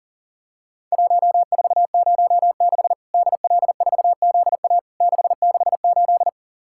Morse code, heard as a keyed tone, about 40 wpm, 700 Hz. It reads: 1406 DL4ZA 668
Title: Worked All Europe DX Contest - QTC Exchange - 40wpm